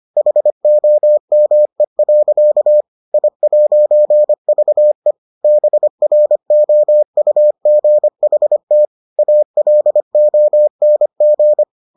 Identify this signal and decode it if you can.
Morse code, heard as a keyed tone, about 25 wpm, 600 Hz. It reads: HOME. I'VE BROUGHT ALONG